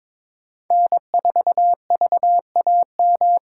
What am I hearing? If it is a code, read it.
Morse code, heard as a keyed tone, about 21 wpm, 700 Hz. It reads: N4VAM
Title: USA Callsign DE Callsign - 5 Characters - 22wpm